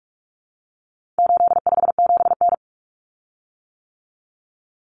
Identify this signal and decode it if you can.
Morse code, heard as a keyed tone, about 44 wpm, 700 Hz. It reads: 857N
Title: Common Cut Numbers - 4-Digit 45wpm